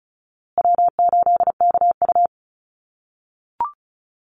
Morse code, heard as a keyed tone, about 34 wpm, 700 Hz. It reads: W8KU
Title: USA Call Signs Repeated 2x 35wpm